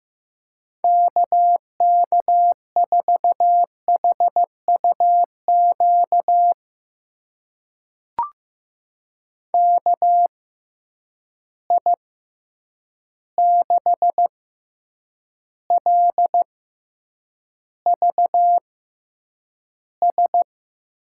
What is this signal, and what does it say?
Morse code, heard as a keyed tone, about 15 wpm, spaced as for 8 wpm, 700 Hz. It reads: KK4HUQ KI6LVS
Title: USA Call Signs - 6 Characters - 6x Character Spacing - 15wpm